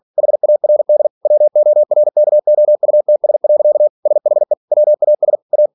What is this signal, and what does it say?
Morse code, heard as a keyed tone, about 46 wpm, 600 Hz. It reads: HARD WORKOUTS. SHE WAS A
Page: Never 47wpm